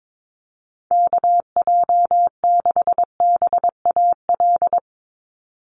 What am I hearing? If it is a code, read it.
Morse code, heard as a keyed tone, about 22 wpm, 700 Hz. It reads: KJ6BAL
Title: USA Call Signs 22wpm